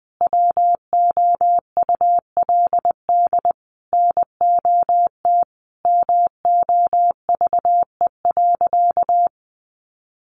Morse code, heard as a keyed tone, about 20 wpm, 700 Hz. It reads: WOULD NOT MOVE.
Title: Sentences from Top 600 Words 20wpm